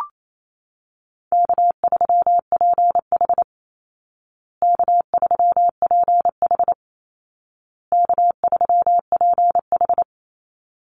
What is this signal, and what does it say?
Morse code, heard as a keyed tone, about 28 wpm, 700 Hz. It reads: K3PH K3PH K3PH
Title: Easy Call Signs Repeated 3x - 28wpm